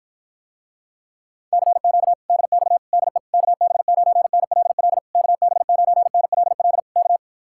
Morse code, heard as a keyed tone, about 53 wpm, 700 Hz. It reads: CQ DX DE KB9NLL KB9NLL K